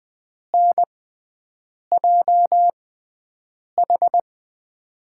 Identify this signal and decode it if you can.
Morse code, heard as a keyed tone, about 20 wpm, spaced as for 7 wpm, 700 Hz. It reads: NJH
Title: USA Call Signs - 5 Characters - 6x Character Spacing - 20wpm